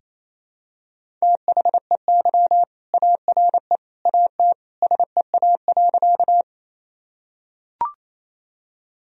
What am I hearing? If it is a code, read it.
Morse code, heard as a keyed tone, about 28 wpm, 700 Hz. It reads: THEY ARE AT SEA.
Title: Sentences from Top 1000 Words - Review Part 1 - 28wpm